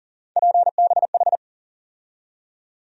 Morse code, heard as a keyed tone, about 40 wpm, 700 Hz. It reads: PBH
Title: USA Call Sign Postfixes - 3 Letters 40wpm